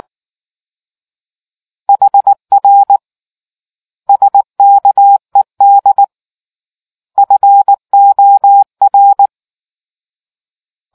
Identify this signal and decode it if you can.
Morse code, heard as a keyed tone, about 19 wpm, 800 Hz. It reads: HR SKED FOR